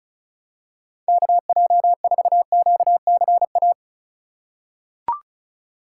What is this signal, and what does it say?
Morse code, heard as a keyed tone, about 36 wpm, 700 Hz. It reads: KJ4QCA
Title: USA Call Signs - 6 Characters - 4x Character Spacing - 35wpm